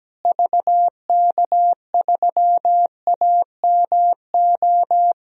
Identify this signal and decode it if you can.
Morse code, heard as a keyed tone, about 17 wpm, 700 Hz. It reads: VK3AMO
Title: DXCC-100 Call Signs - 6 Characters - Repeated 3x - 17wpm